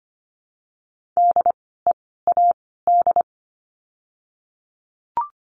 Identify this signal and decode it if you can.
Morse code, heard as a keyed tone, about 25 wpm, 700 Hz. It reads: D E A D
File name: course / 25:10wpm / 094 Lesson 16f - Letter c - Words 25wpm